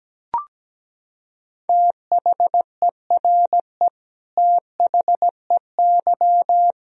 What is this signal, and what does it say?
Morse code, heard as a keyed tone, about 17 wpm, 700 Hz. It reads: THERE THEY
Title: Sentences from Top 100 Words and Abbreviations 17wpm